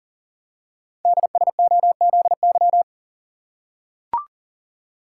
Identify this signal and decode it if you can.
Morse code, heard as a keyed tone, about 40 wpm, 700 Hz. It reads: DSOZY
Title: Code Groups - 5 Letters - 40wpm